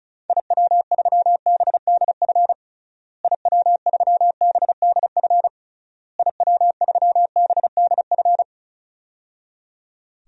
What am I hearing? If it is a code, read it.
Morse code, heard as a keyed tone, about 36 wpm, 700 Hz. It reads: IW3BDF IW3BDF IW3BDF
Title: DXCC-100 Call Signs - 6 Characters - Repeated 3x - 35wpm